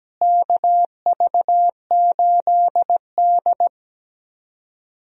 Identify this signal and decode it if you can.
Morse code, heard as a keyed tone, about 17 wpm, 700 Hz. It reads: KV8D
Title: USA Call Signs - 4 Characters - Repeated 2x - 17wpm